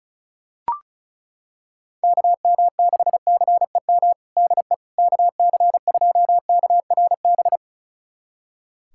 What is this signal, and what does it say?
Morse code, heard as a keyed tone, about 34 wpm, 700 Hz. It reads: KM6CEM DE KC2KRB
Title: USA Callsign DE Callsign - 6 Characters - 35wpm